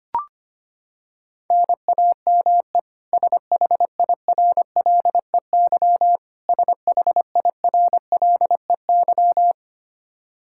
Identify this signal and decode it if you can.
Morse code, heard as a keyed tone, about 25 wpm, 700 Hz. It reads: NAME SHIRLEY SHIRLEY
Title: K1USN SST QSO Practice 25wpm